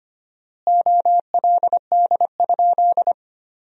Morse code, heard as a keyed tone, about 25 wpm, 700 Hz. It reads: OLD?
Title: 4x Word-Spacing - Sentences from Top 400 Words - 25wpm